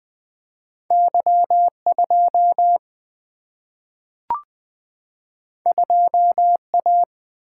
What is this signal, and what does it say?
Morse code, heard as a keyed tone, about 20 wpm, 700 Hz. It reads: Y 2 2 A
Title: Single-Letter Single-Number 20wpm